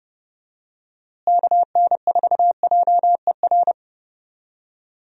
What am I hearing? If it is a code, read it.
Morse code, heard as a keyed tone, about 30 wpm, 700 Hz. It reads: KN4JER
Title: USA Call Signs - 6 Characters - Repeated 2x - 30wpm